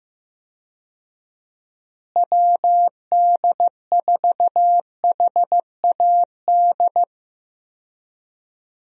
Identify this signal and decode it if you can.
Morse code, heard as a keyed tone, about 15 wpm, 700 Hz. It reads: WD4HAD
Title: USA Call Signs - 6 Characters - Repeated 3x - 15wpm